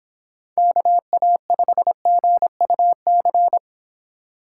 Morse code, heard as a keyed tone, about 26 wpm, 700 Hz. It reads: KA5GUC